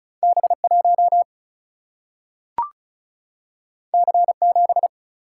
Morse code, heard as a keyed tone, about 36 wpm, 700 Hz. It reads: B 1 C 7
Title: Single-Letter Single-Number 35wpm